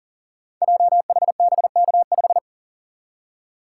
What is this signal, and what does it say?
Morse code, heard as a keyed tone, about 38 wpm, 700 Hz. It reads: JHBK5
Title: Code Groups - 5 Letter:Numbers - 40wpm